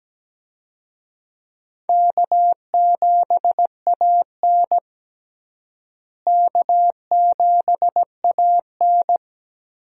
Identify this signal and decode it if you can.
Morse code, heard as a keyed tone, about 17 wpm, 700 Hz. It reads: K7AN K7AN
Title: USA Call Signs - 4 Characters - Repeated 2x - 17wpm